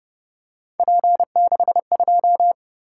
Morse code, heard as a keyed tone, about 31 wpm, 700 Hz. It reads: P62
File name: Speed-Racing - Three Letter-Number Combinations - 25wpm